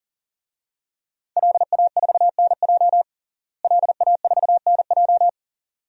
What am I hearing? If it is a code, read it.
Morse code, heard as a keyed tone, about 37 wpm, 700 Hz. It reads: LA4NJ LA4NJ
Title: DXCC-100 Call Signs - 5 Characters - Repeated 2x - 40wpm